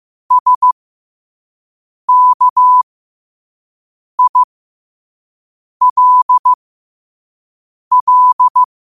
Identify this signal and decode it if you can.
Morse code, heard as a keyed tone, about 15 wpm, spaced as for 5 wpm, 1000 Hz. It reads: SKILL
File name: qotdfiles / 5wpm